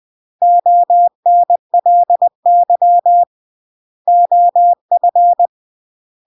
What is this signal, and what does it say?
Morse code, heard as a keyed tone, about 20 wpm, 700 Hz. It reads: ONLY OF
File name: CW / jodorowsky / 82 commandments 20-20